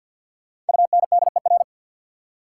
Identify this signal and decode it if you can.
Morse code, heard as a keyed tone, about 50 wpm, 700 Hz. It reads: UNDER
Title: Sets of 2 Words - New Word Drill - 50wpm